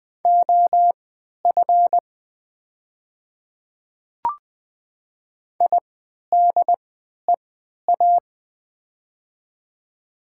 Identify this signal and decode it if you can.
Morse code, heard as a keyed tone, about 20 wpm, spaced as for 11 wpm, 700 Hz. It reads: OF IDEA
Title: 3x Character Spacing - Top 300 Words - 20wpm